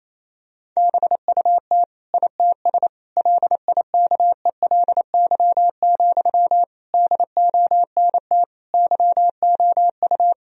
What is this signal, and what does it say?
Morse code, heard as a keyed tone, about 28 wpm, 700 Hz. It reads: BUT ITS LIKELY, DONT YOU
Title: Sentences from Top 1000 Words - Review Part 4 - 28wpm